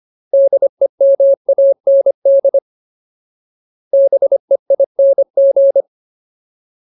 Morse code, heard as a keyed tone, about 25 wpm, 550 Hz. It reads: DEMAND BEING
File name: Ultimate Word List / 25WPM Ultimate